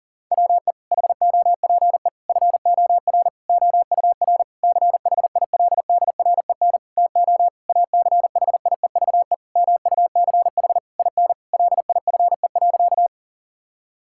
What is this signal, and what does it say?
Morse code, heard as a keyed tone, about 37 wpm, 700 Hz. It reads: WE HOPE FOR OUR CHILDREN TO ACHIEVE MUCH IN LIFE.
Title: Sentences from Top 1000 Words - Review Part 6 - 40wpm